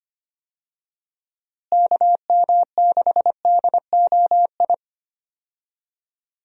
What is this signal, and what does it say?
Morse code, heard as a keyed tone, about 25 wpm, 700 Hz. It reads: KM6DOI
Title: USA Call Signs - 6 Characters - Repeated 3x - 25wpm